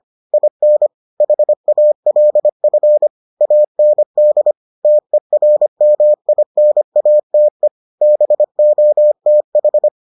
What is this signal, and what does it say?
Morse code, heard as a keyed tone, about 25 wpm, 600 Hz. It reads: IN HALF AND TERMINATE BOTH